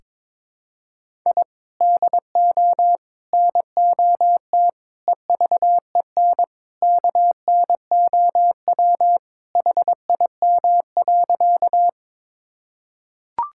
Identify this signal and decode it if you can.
Morse code, heard as a keyed tone, about 22 wpm, 700 Hz. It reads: I DO NOT EVEN KNOW HIM.
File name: practice / 22wpm / Sentences with 4-Word Phrases - 22wpm